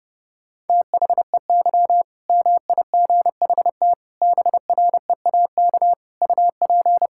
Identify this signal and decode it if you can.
Morse code, heard as a keyed tone, about 32 wpm, 700 Hz. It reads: THEY MIGHT BREAK UP
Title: Sentences from Top 1000 Words - Review Part 6 - 30wpm